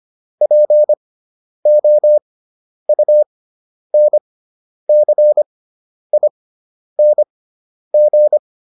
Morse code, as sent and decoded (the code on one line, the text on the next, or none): .--. --- ..- -. -.-. .. -. --.
POUNCING